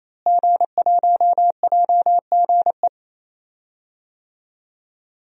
Morse code, heard as a keyed tone, about 28 wpm, 700 Hz. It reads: G1JGE
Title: DXCC-100 Call Signs - 5 Characters - Repeated 3x - 28wpm